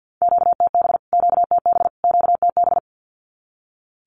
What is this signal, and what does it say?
Morse code, heard as a keyed tone, about 50 wpm, 700 Hz. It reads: QTB QTB QTB
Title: Q-Codes Repeated 3x 50wpm